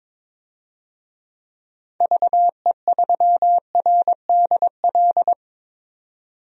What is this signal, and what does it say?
Morse code, heard as a keyed tone, about 21 wpm, 700 Hz. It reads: VE3RDL
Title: DXCC-100 Call Signs - 6 Characters - 22wpm